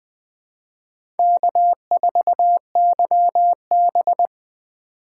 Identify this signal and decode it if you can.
Morse code, heard as a keyed tone, about 20 wpm, 700 Hz. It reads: K4YB
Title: USA Call Signs - 4 Characters - 2x Character Spacing - 20wpm